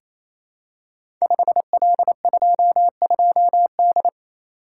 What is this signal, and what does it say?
Morse code, heard as a keyed tone, about 27 wpm, 700 Hz. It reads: 5L22D
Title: Code Groups - 5 Letter:Numbers - 28wpm